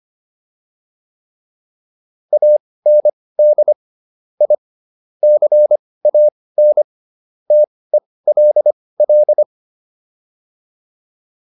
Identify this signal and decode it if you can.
Morse code, heard as a keyed tone, about 25 wpm, spaced as for 18 wpm, 600 Hz. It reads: AND I CAN TELL